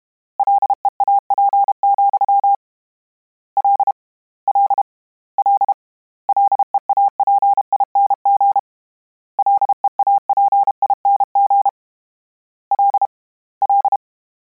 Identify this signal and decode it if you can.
Morse code, heard as a keyed tone, about 32 wpm, 800 Hz. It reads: LEAP, L L L LEAPING LEAPING L L
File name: Combinations / Combinations-mp3s / Combinations@30